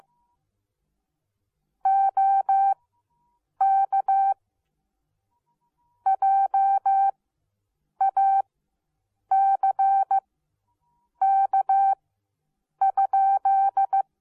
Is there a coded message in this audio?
Yes: Morse code from 1.9 to 14.0 s, heard as a keyed tone, about 15 wpm, spaced as for 7 wpm, 780 Hz. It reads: OK JACK?